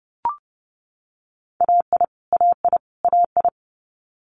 Morse code, heard as a keyed tone, about 30 wpm, 690 Hz. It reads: AI AI AI
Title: Multi-letter Phonemes Repeated 3x 30wpm